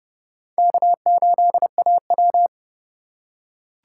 Morse code, heard as a keyed tone, about 30 wpm, 700 Hz. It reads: K8AW